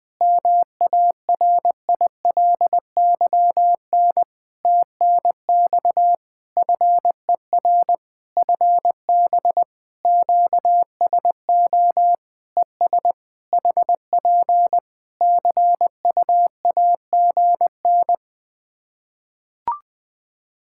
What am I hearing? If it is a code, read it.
Morse code, heard as a keyed tone, about 20 wpm, 700 Hz. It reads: MARILYN TNX FER FB QSO ES HP CUAGN
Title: Practice QSO - Exchange 4 - Ending - 20wpm